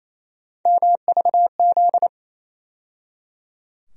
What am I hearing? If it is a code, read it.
Morse code, heard as a keyed tone, about 27 wpm, 700 Hz. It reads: MVZ
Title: USA Call Sign Postfixes - 3 Letters 28wpm